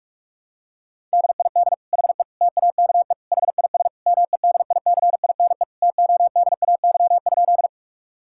Morse code, heard as a keyed tone, about 46 wpm, 700 Hz. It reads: DID HE TAKE HIS MEDICINE TODAY?
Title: Sentences from Top 1000 Words - Review Part 6 - 45wpm